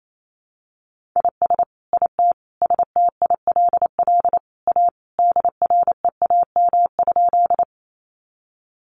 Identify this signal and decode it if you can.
Morse code, heard as a keyed tone, about 28 wpm, 700 Hz. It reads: IS IT STILL A DREAM?
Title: Sentences from Top 1500 Words 28wpm